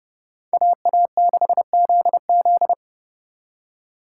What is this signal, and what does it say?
Morse code, heard as a keyed tone, about 38 wpm, 700 Hz. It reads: AA6ZZ